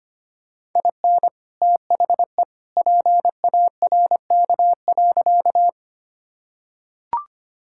Morse code, heard as a keyed tone, about 25 wpm, 700 Hz. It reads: IN THE PARK.
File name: practice / 25wpm / Sentences with 2-Word Phrases - 25wpm